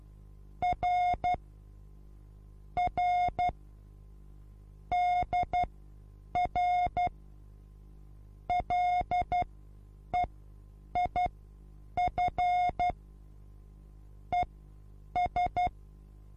Morse code, heard as a keyed tone, about 12 wpm, spaced as for 8 wpm, 720 Hz. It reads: R R DR LEIF ES
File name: repos / Morsekurs / DJ4UF Morsen23